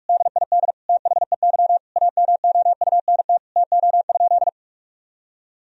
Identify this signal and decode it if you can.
Morse code, heard as a keyed tone, about 46 wpm, 700 Hz. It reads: DID THEY AMOUNT TO?
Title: Sentences from Top 1000 Words - Review Part 4 - 45wpm